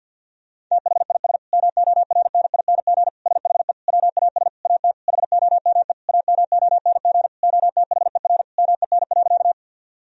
Morse code, heard as a keyed tone, about 50 wpm, 700 Hz. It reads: THIS MORNING SHE WAS AT HOME AMONG OTHER MEN.
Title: Sentences from Top 1000 Words - Review Part 3 - 50wpm